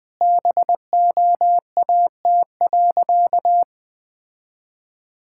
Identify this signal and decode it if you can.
Morse code, heard as a keyed tone, about 21 wpm, 700 Hz. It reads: BOAT.
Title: Sentences from Top 1000 Words - Review Part 6 - 20wpm